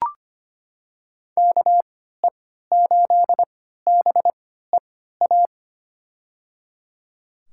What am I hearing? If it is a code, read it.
Morse code, heard as a keyed tone, about 25 wpm, spaced as for 14 wpm, 700 Hz. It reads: KE8BEA
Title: USA Call Signs - 6 Characters - 3x Character Spacing - 25wpm